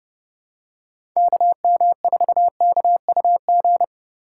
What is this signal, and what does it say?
Morse code, heard as a keyed tone, about 30 wpm, 700 Hz. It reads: KM4KUG